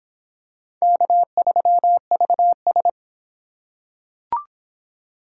Morse code, heard as a keyed tone, about 22 wpm, 700 Hz. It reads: K3VS